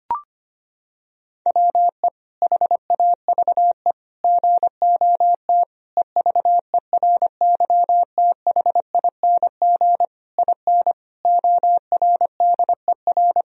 Morse code, heard as a keyed tone, about 25 wpm, 700 Hz. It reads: WE HAVE GOT EVERYTHING IN ORDER
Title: Sentences from Top 1000 Words - Review Part 3 - 25wpm